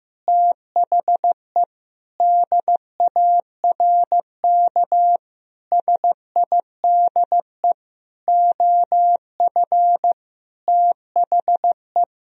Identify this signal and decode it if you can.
Morse code, heard as a keyed tone, about 15 wpm, 700 Hz. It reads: THE DARK SIDE OF THE
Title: Sentences from Top 1000 Words - Review Part 3 - 15wpm